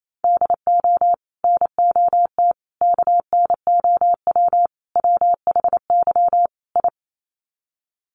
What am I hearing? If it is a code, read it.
Morse code, heard as a keyed tone, about 28 wpm, 700 Hz. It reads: DO NOT KNOW WHY I
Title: Sets of 5 Words Encore 28wpm